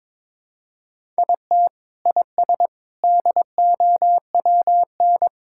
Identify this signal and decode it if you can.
Morse code, heard as a keyed tone, about 22 wpm, 700 Hz. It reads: IT IS DOWN
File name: practice / 22wpm / Sentences with 3-Word Phrases - 22wpm